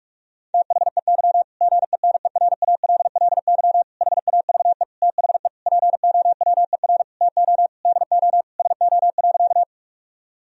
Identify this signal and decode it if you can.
Morse code, heard as a keyed tone, about 47 wpm, 700 Hz. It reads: THEY GENERALLY HAVE THE POWER TO DO SO.
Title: Sentences from Top 1000 Words - Review Part 4 - 45wpm